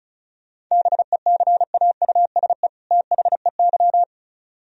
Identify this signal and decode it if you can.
Morse code, heard as a keyed tone, about 38 wpm, 700 Hz. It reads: BECAUSE THEY